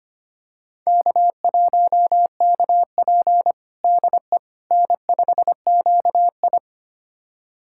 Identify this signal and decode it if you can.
Morse code, heard as a keyed tone, about 25 wpm, 700 Hz. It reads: K1KP DE N5QI
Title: USA Callsign DE Callsign - 4 Characters - 25wpm